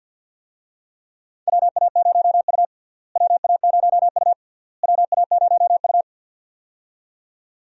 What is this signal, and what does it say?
Morse code, heard as a keyed tone, about 52 wpm, 700 Hz. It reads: WA0U WA0U WA0U
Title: USA Call Signs - 4 Characters - Repeated 3x - 50wpm